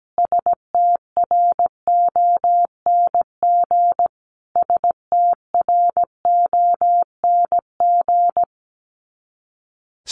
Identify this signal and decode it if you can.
Morse code, heard as a keyed tone, about 17 wpm, 700 Hz. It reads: STRONG STRONG
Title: Top 500 Words Repeated 17wpm